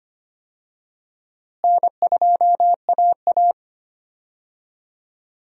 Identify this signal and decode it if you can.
Morse code, heard as a keyed tone, about 25 wpm, 700 Hz. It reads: N2AA
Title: USA Call Signs - 4 Characters - Repeated 3x - 25wpm